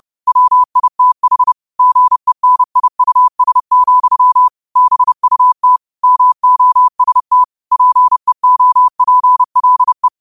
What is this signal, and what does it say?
Morse code, heard as a keyed tone, about 30 wpm, 1000 Hz. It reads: WITH GENIUS, BUT MOST PEOPLE
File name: qotdfiles / 30wpm